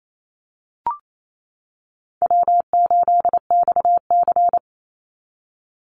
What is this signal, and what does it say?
Morse code, heard as a keyed tone, about 28 wpm, 700 Hz. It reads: W8XC
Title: Easy Call Signs - 28wpm